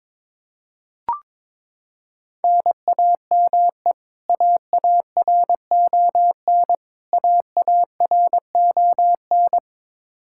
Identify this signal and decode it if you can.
Morse code, heard as a keyed tone, about 22 wpm, 700 Hz. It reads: NAME AARON AARON
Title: Practice QSO - Exchange 1 - Beginning - 22wpm